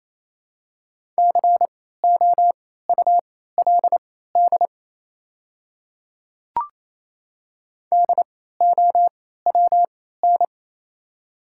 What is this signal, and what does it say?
Morse code, heard as a keyed tone, about 28 wpm, spaced as for 16 wpm, 700 Hz. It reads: COULD DOWN
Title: 3x Character Spacing - Top 100 Words - 28wpm